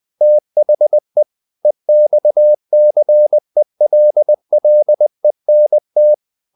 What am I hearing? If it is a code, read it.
Morse code, heard as a keyed tone, about 20 wpm, 600 Hz. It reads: THE EXCELLENT
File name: the old man and the sea / the old man and the sea part 1 20wpm